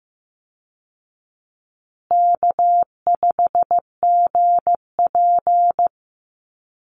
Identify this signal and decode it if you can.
Morse code, heard as a keyed tone, about 14 wpm, 700 Hz. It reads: K5GP
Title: Call Signs Repeated 2x - 15wpm